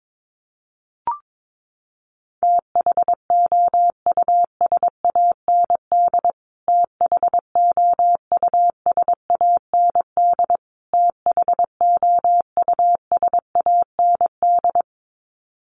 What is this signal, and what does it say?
Morse code, heard as a keyed tone, about 22 wpm, 700 Hz. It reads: THOUSAND THOUSAND THOUSAND
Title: Top 500 Words Repeated 22wpm